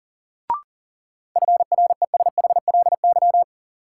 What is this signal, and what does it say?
Morse code, heard as a keyed tone, about 40 wpm, 700 Hz. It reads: FRESHLY
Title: Sets of 5 Words Encore - New Word Drill - 40wpm